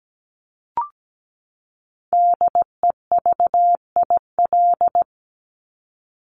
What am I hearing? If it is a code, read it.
Morse code, heard as a keyed tone, about 17 wpm, 700 Hz. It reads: DEVIL
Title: Top 1751-2000 Words 17wpm